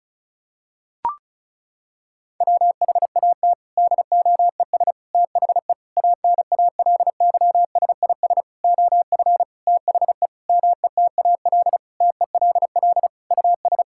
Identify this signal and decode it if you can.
Morse code, heard as a keyed tone, about 36 wpm, 700 Hz. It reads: WHAT DOES THE ANALYSIS OF THE METAL TELL US
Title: Sentences from Top 1000 Words - Review Part 6 - 35wpm